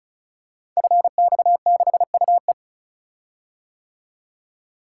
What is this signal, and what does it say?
Morse code, heard as a keyed tone, about 36 wpm, 700 Hz. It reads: FX6UE
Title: Code Groups - 5 Letter:Numbers - 35wpm